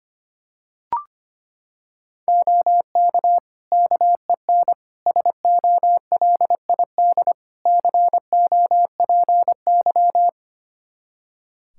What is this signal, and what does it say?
Morse code, heard as a keyed tone, about 25 wpm, 700 Hz. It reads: OK KEN SOLID COPY
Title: Practice QSO - Exchange 3 - More Info - 25wpm